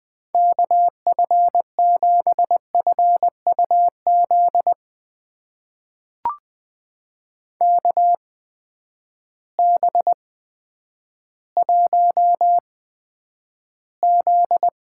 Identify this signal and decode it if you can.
Morse code, heard as a keyed tone, about 20 wpm, spaced as for 10 wpm, 700 Hz. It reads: KF7FUZ KB1Z
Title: USA Call Signs - 6 Characters - 8x Character Spacing - 20wpm